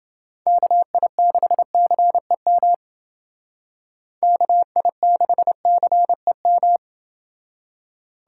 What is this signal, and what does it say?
Morse code, heard as a keyed tone, about 29 wpm, 700 Hz. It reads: KI6CEM KI6CEM